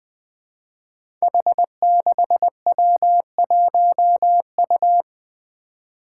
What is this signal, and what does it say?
Morse code, heard as a keyed tone, about 20 wpm, 700 Hz. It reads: H6W1U
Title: Five Challenging Letter-Number Combinations 20wpm